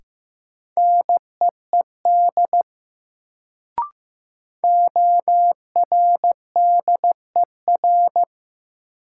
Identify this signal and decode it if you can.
Morse code, heard as a keyed tone, about 15 wpm, 700 Hz. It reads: NEED ORDER
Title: Sets of 2 Words - New Word Drill - 15wpm